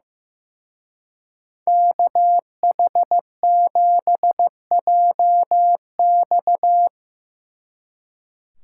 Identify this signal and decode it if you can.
Morse code, heard as a keyed tone, about 17 wpm, 700 Hz. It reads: KH7JX
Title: Speed Racing - USA 2x2 Call Signs 15wpm